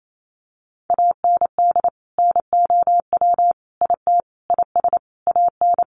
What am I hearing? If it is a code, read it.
Morse code, heard as a keyed tone, about 28 wpm, 700 Hz. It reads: AND NOW IT IS AN
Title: Sentences from Top 1000 Words 28wpm